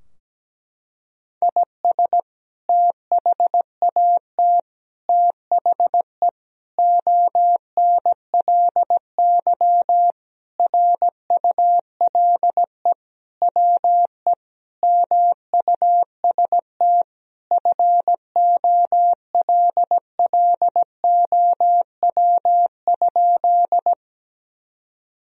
Morse code, heard as a keyed tone, about 17 wpm, 700 Hz. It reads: IS THAT THE ONLY RULE WE MUST FOLLOW?
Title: Sentences from Top 1000 Words - Review Part 2 - 17wpm